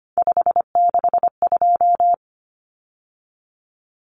Morse code, heard as a keyed tone, about 25 wpm, 700 Hz. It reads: 562
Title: Numbers - 3-Digit 25wpm